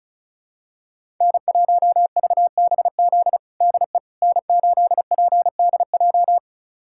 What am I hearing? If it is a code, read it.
Morse code, heard as a keyed tone, about 34 wpm, 700 Hz. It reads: N1VBZ DE N8PDJ
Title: USA Callsign DE Callsign - 5 Characters - 35wpm